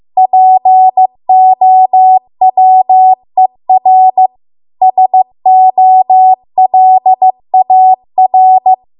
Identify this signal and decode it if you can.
Morse code, heard as a keyed tone, about 15 wpm, 750 Hz. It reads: POWER SOLAR